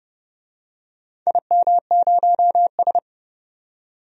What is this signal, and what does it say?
Morse code, heard as a keyed tone, about 31 wpm, 700 Hz. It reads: IM0S
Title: DXCC-100 Call Signs - 4 Characters - 30wpm